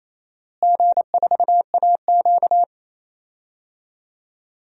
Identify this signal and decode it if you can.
Morse code, heard as a keyed tone, about 28 wpm, 700 Hz. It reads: G4AQ
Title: DXCC-100 Call Signs - 4 Characters - Repeated 2x - 28wpm